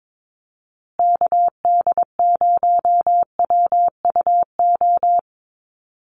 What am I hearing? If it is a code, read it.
Morse code, heard as a keyed tone, about 22 wpm, 700 Hz. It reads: KD0WUO
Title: USA Call Signs Repeated 3x 22wpm